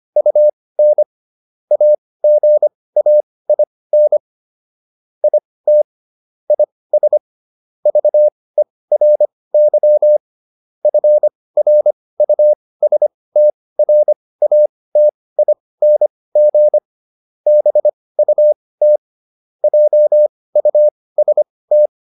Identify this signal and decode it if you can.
Morse code, heard as a keyed tone, about 25 wpm, spaced as for 18 wpm, 600 Hz. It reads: UN AGAIN IT IS VERY FRUSTRATING BUT JUST